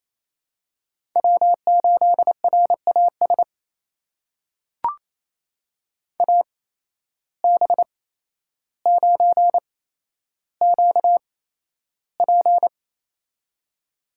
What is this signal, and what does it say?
Morse code, heard as a keyed tone, about 28 wpm, spaced as for 12 wpm, 700 Hz. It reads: W8RAS AB9QP
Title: USA Call Signs - 5 Characters - 8x Character Spacing - 28wpm